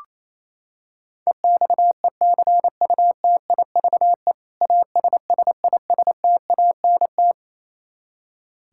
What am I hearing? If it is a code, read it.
Morse code, heard as a keyed tone, about 28 wpm, 700 Hz. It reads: EXECUTIVE ASSISTANT
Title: Practice QSO - Exchange 3 - More Info - 28wpm